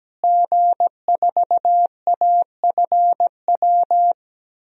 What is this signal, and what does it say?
Morse code, heard as a keyed tone, about 17 wpm, 700 Hz. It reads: G4AFW
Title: DXCC-100 Call Signs - 5 Characters - Repeated 3x - 17wpm